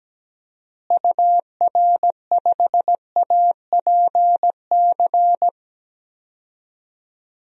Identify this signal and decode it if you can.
Morse code, heard as a keyed tone, about 17 wpm, 700 Hz. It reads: UR5APC
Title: DXCC-100 Call Signs - 6 Characters - Repeated 2x - 17wpm